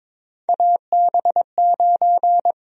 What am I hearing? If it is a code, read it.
Morse code, heard as a keyed tone, about 22 wpm, 700 Hz. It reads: AB9
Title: USA Call Sign Prefixes - Repeated 2x - 22wpm